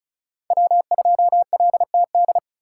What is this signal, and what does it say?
Morse code, heard as a keyed tone, about 36 wpm, 700 Hz. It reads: W2LTD
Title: USA Call Signs - 2x Character Spacing - 35wpm